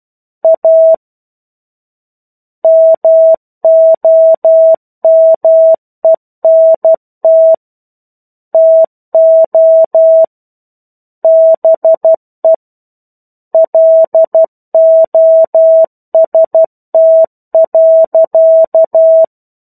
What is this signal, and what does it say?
Morse code, heard as a keyed tone, about 12 wpm, 650 Hz. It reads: A MOMENT TO BE LOST.
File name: 01-Chapter-01 12 WPM